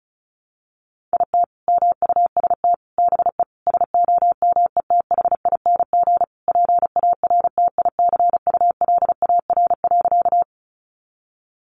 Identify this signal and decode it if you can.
Morse code, heard as a keyed tone, about 34 wpm, 700 Hz. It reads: IT MUST BE SOMETHING PARTICULAR.
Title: Sentences from Top 600 Words 35wpm